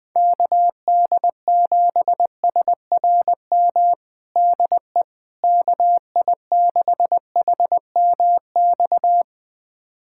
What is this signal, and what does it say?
Morse code, heard as a keyed tone, about 20 wpm, 700 Hz. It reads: KD7SRM DE KI6HMX
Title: USA Callsign DE Callsign 20wpm